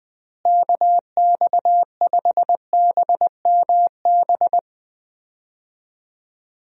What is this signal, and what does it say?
Morse code, heard as a keyed tone, about 20 wpm, 700 Hz. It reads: KX5BMB
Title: USA Call Signs - 6 Characters - Repeated 2x - 20wpm